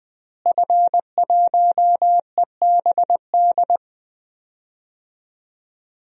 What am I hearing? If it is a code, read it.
Morse code, heard as a keyed tone, about 20 wpm, 700 Hz. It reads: F1EBD
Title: DXCC-100 Call Signs - 5 Characters - Repeated 3x - 20wpm